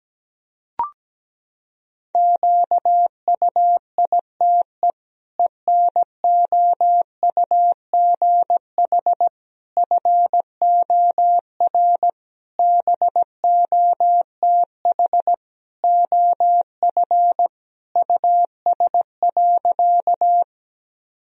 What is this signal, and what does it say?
Morse code, heard as a keyed tone, about 17 wpm, 700 Hz. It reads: QUITE ENOUGH FOR BOTH OF US.
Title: Sentences from Top 1000 Words - Review Part 1 - 17wpm